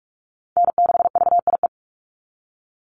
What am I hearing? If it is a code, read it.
Morse code, heard as a keyed tone, about 46 wpm, 690 Hz. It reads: N6VIE
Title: USA Call Signs 45wpm